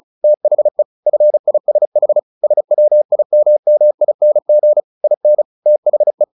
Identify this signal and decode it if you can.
Morse code, heard as a keyed tone, about 36 wpm, 600 Hz. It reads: THE FISH SWIMMING IN THE
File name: the old man and the sea / the old man and the sea 1 35wpm